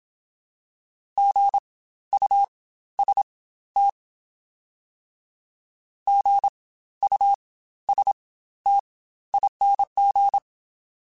Morse code, heard as a keyed tone, about 27 wpm, spaced as for 14 wpm, 800 Hz. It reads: GUST GUSTING
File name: Spaced Alphas / Spaced Alphas mp3s / Gs sp @25